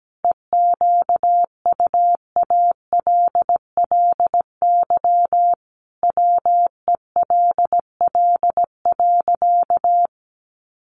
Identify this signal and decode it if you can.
Morse code, heard as a keyed tone, about 17 wpm, 700 Hz. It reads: EQUALLY WELL.
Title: Sentences from Top 1400 Words 17wpm